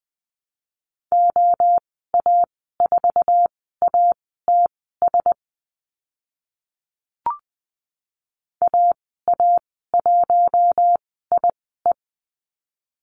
Morse code, heard as a keyed tone, about 20 wpm, spaced as for 15 wpm, 700 Hz. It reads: OA4ATS AA1IE
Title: 082 Lesson 14g - Number 2 - Call Signs 20wpm